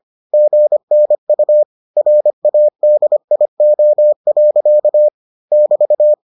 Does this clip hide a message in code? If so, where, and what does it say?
Yes: Morse code from 0.3 to 6.1 s, heard as a keyed tone, about 25 wpm, 600 Hz. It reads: GNU RADIO. <BT>